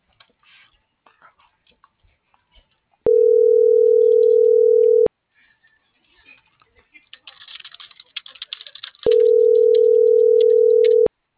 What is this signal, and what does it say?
Ringback tone (North America), 440 + 480 Hz.